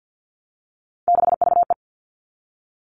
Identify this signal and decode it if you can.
Morse code, heard as a keyed tone, about 50 wpm, 700 Hz. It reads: 6VE
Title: Three Challenging Letter-Number Combinations 50wpm